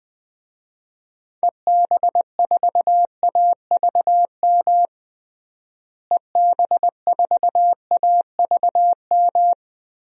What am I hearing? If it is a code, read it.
Morse code, heard as a keyed tone, about 21 wpm, 700 Hz. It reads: EB4AVM EB4AVM
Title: DXCC-100 Call Signs - 6 Characters - Repeated 2x - 20wpm